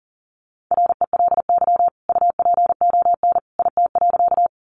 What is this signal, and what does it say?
Morse code, heard as a keyed tone, about 40 wpm, 700 Hz. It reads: RELY UPON IT.
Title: Sentences from Top 1200 Words 40wpm